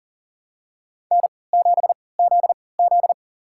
Morse code, heard as a keyed tone, about 40 wpm, spaced as for 23 wpm, 700 Hz. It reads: N7ZZ
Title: USA Call Signs - 4 Characters - 3x Character Spacing - 40wpm